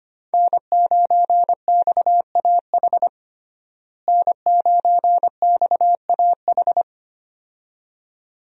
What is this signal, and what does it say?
Morse code, heard as a keyed tone, about 25 wpm, 700 Hz. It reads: N9XAH N9XAH
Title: USA Call Signs - 5 Characters - Repeated 2x - 25wpm